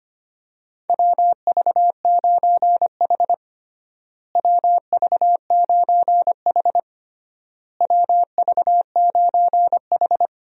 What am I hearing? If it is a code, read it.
Morse code, heard as a keyed tone, about 25 wpm, 700 Hz. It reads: WV9H WV9H WV9H
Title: USA Call Signs - 4 Characters - Repeated 3x - 25wpm